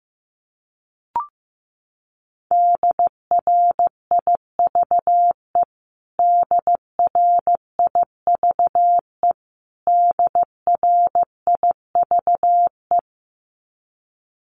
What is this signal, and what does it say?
Morse code, heard as a keyed tone, about 15 wpm, 700 Hz. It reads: DRIVE DRIVE DRIVE
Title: Top 500 Words Repeated 15wpm